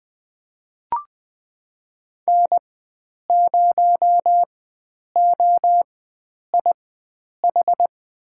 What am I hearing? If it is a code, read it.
Morse code, heard as a keyed tone, about 20 wpm, spaced as for 9 wpm, 700 Hz. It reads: N0OIH
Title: USA Call Signs - 5 Characters - 4x Character Spacing - 20wpm